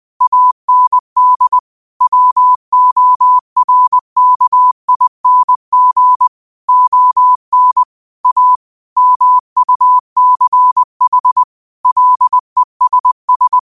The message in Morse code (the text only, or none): AND WORKING ON A MUCH LESS